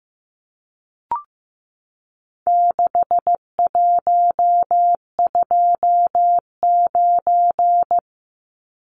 Morse code, heard as a keyed tone, about 15 wpm, 700 Hz. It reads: 6129
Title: Numbers - 4-Digit 15wpm